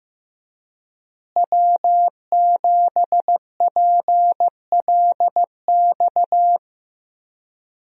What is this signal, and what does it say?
Morse code, heard as a keyed tone, about 15 wpm, 700 Hz. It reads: W7PLX
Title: USA Call Signs - 5 Characters - Repeated 3x - 15wpm